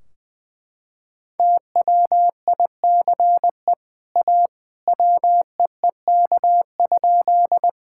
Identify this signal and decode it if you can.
Morse code, heard as a keyed tone, about 20 wpm, 700 Hz. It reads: TWICE A WEEK?
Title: Sentences from Top 1600 Words 20wpm